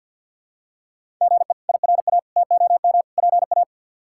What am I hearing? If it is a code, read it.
Morse code, heard as a keyed tone, about 47 wpm, 700 Hz. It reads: GE IRA TOM PA
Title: K1USN SST QSO Practice - 50wpm